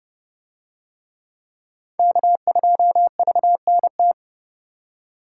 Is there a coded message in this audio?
Yes: Morse code from 2.0 to 4.1 s, heard as a keyed tone, about 27 wpm, 700 Hz. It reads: K2VNT